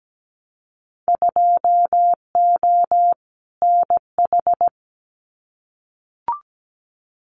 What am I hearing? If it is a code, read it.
Morse code, heard as a keyed tone, about 17 wpm, 700 Hz. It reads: 2O NH
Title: Winter Field Day Exchanges 17wpm